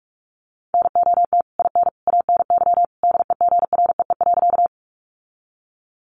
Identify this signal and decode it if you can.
Morse code, heard as a keyed tone, about 44 wpm, 700 Hz. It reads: NOT IN ANY DEGREE.
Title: Sentences from Top 1500 Words 45wpm